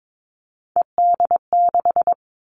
Morse code, heard as a keyed tone, about 22 wpm, 700 Hz. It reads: ED6
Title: Callsign Prefixes for Top 100 DXCC - 22wpm